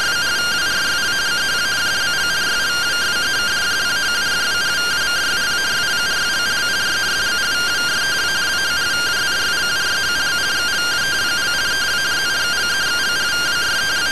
Two-tone teleprinter signal, about 170 Hz shift.